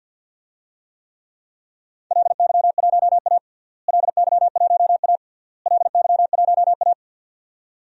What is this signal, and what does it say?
Morse code, heard as a keyed tone, about 46 wpm, 700 Hz. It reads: LY1A LY1A LY1A
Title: DXCC-100 Call Signs - 4 Characters - Repeated 3x - 50wpm